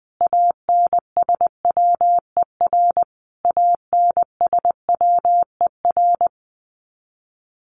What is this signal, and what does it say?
Morse code, heard as a keyed tone, about 20 wpm, 700 Hz. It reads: ANSWER ANSWER
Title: Top 500 Words Repeated 20wpm